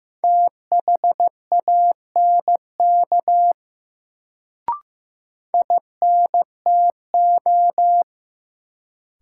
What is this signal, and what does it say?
Morse code, heard as a keyed tone, about 15 wpm, 700 Hz. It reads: THANK INTO
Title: Sentences from Top 600 Words - New Word Drill - 15wpm